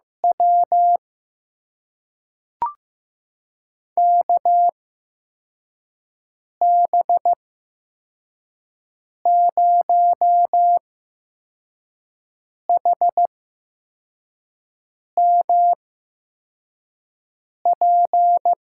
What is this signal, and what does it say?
Morse code, heard as a keyed tone, about 15 wpm, spaced as for 4 wpm, 700 Hz. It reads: W KB0HMP
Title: USA Call Signs - 6 Characters - 8x Character Spacing - 15wpm